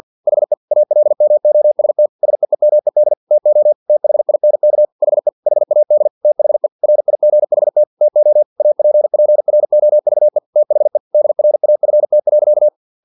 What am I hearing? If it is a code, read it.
Morse code, heard as a keyed tone, about 50 wpm, 600 Hz. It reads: HE ALMOST SEEMED TO THINK HE HAD THE RIGHT TO APPROVE THE DRAFT.
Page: Never 49wpm